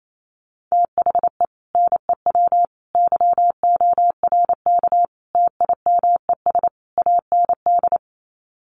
Morse code, heard as a keyed tone, about 28 wpm, 700 Hz. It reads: THE NEW YORK TIMES AND
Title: Sets of 5 Words Encore 28wpm